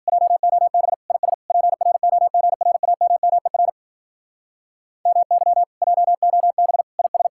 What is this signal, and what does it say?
Morse code, heard as a keyed tone, about 51 wpm, 700 Hz. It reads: JOB IS PROGRAMMER MY JOB IS